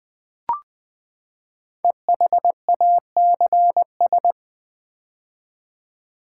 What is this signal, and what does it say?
Morse code, heard as a keyed tone, about 21 wpm, 700 Hz. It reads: EHACS
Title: Code Groups - 5 Letters - 20wpm